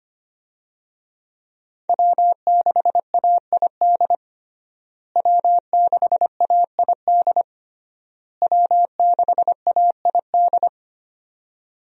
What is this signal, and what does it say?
Morse code, heard as a keyed tone, about 25 wpm, 700 Hz. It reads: W6AID W6AID W6AID
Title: USA Call Signs - 5 Characters - Repeated 3x - 25wpm